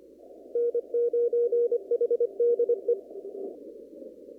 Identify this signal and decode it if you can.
Morse code, heard as a keyed tone, about 24 wpm, 480 Hz. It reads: N9HDE
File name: CW / kt5j wpx cw-op-k5pi